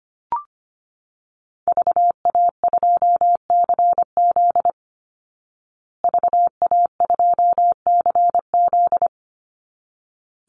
Morse code, heard as a keyed tone, about 25 wpm, 700 Hz. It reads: VA2CZ VA2CZ
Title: Call Signs Repeated 3x - 25wpm